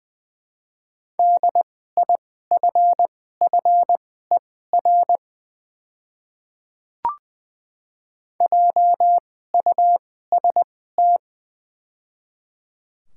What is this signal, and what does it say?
Morse code, heard as a keyed tone, about 21 wpm, spaced as for 15 wpm, 700 Hz. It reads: DIFFER JUST
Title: 2x Character Spacing - Top 200 Words - 20wpm